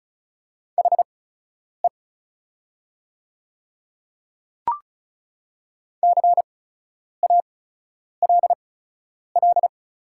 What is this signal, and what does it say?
Morse code, heard as a keyed tone, about 36 wpm, spaced as for 10 wpm, 700 Hz. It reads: HE CALL
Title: 8x Character Spacing - Top 100 Words - 35wpm